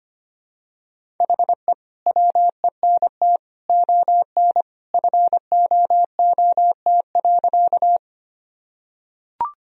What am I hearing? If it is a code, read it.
Morse code, heard as a keyed tone, about 25 wpm, 700 Hz. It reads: HE WENT ON FOOT.
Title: Sentences from Top 1000 Words - Review Part 3 - 25wpm